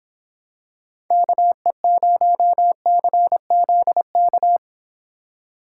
Morse code, heard as a keyed tone, about 26 wpm, 700 Hz. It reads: KE0CZK